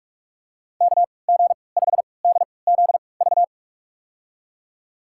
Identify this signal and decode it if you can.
Morse code, heard as a keyed tone, about 46 wpm, spaced as for 26 wpm, 700 Hz. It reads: KG5DZV
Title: USA Call Signs - 3x Character Spacing - 45wpm